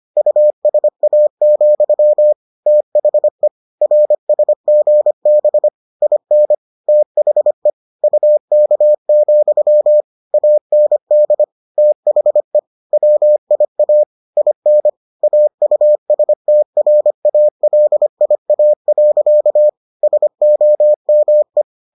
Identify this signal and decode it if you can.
Morse code, heard as a keyed tone, about 25 wpm, 600 Hz. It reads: USA, THE RSGB IN THE UK, AND THE WIA IN AUSTRALIA. SOME